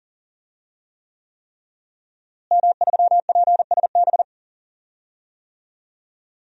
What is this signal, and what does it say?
Morse code, heard as a keyed tone, about 40 wpm, 700 Hz. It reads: M3PSB
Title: DXCC-100 Call Signs - 5 Characters - Repeated 3x - 40wpm